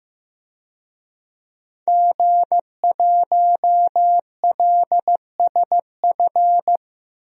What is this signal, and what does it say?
Morse code, heard as a keyed tone, about 15 wpm, 700 Hz. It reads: G1LSF
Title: DXCC-100 Call Signs - 5 Characters - 15wpm